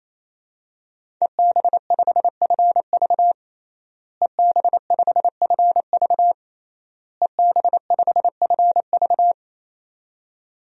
Morse code, heard as a keyed tone, about 27 wpm, 700 Hz. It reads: EB5FV EB5FV EB5FV
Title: DXCC-100 Call Signs - 5 Characters - Repeated 3x - 28wpm